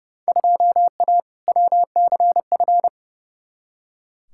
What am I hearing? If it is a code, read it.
Morse code, heard as a keyed tone, about 32 wpm, 700 Hz. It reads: 2A WCF
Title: Field Day Exchanges Encore - 30wpm